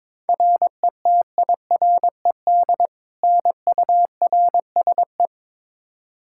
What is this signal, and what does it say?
Morse code, heard as a keyed tone, about 22 wpm, 700 Hz. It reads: RETIRED NURSE
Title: Practice QSO - Exchange 3 - More Info - 22wpm